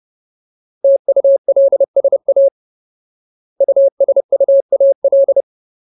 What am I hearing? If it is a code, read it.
Morse code, heard as a keyed tone, about 30 wpm, 550 Hz. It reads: TULSA USUAL
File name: Ultimate Word List / 30WPM Ultimate